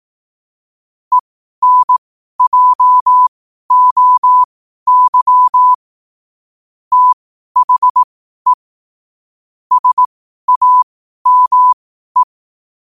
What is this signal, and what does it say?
Morse code, heard as a keyed tone, about 18 wpm, spaced as for 12 wpm, 1000 Hz. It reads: ENJOY THE SAME
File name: qotdfiles / 12wpm